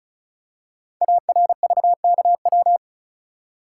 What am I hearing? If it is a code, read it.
Morse code, heard as a keyed tone, about 36 wpm, 690 Hz. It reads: ARVKW
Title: Code Groups - 5 Letters - 35wpm